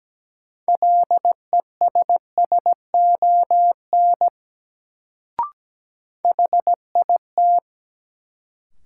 Morse code, heard as a keyed tone, about 17 wpm, 700 Hz. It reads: LESSON HIT
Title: Sentences from Top 1200 Words - New Word Drill - 17wpm